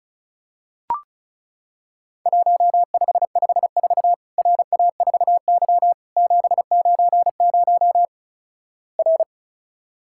Morse code, heard as a keyed tone, about 36 wpm, 700 Hz. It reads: 1554 RA4Y 790 R
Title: Worked All Europe DX Contest - QTC Exchange - 35wpm